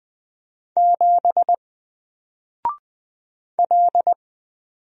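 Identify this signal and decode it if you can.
Morse code, heard as a keyed tone, about 20 wpm, 700 Hz. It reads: 7 L
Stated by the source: Single letter and number Morse drill at 20 wpm, from N4LCD.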